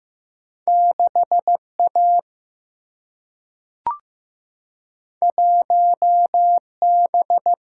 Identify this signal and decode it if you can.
Morse code, heard as a keyed tone, about 15 wpm, 700 Hz. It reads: 6 A 1 B
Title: Single-Letter Single-Number 15wpm